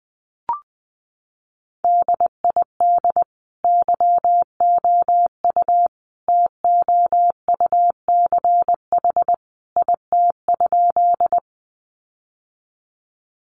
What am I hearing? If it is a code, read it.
Morse code, heard as a keyed tone, about 20 wpm, 700 Hz. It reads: DID YOU TOUCH IT?
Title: Sentences from Top 1200 Words 20wpm